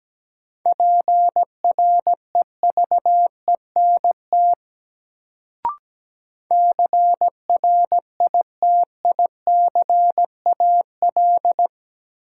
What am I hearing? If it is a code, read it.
Morse code, heard as a keyed tone, about 17 wpm, 700 Hz. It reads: PREVENT CRITICAL
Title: Sentences from Top 1100 Words - New Word Drill - 17wpm